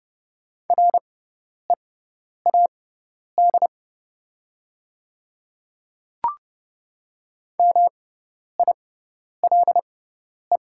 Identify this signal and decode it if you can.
Morse code, heard as a keyed tone, about 30 wpm, spaced as for 10 wpm, 690 Hz. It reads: READ MILE
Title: 6x Character Spacing - Top 300 Words - 30wpm